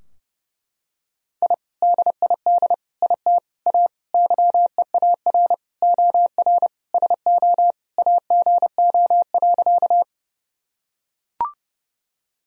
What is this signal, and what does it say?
Morse code, heard as a keyed tone, about 30 wpm, 700 Hz. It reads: I DID IT A YEAR OR SO AGO.
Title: Sentences from Top 1000 Words - Review Part 3 - 30wpm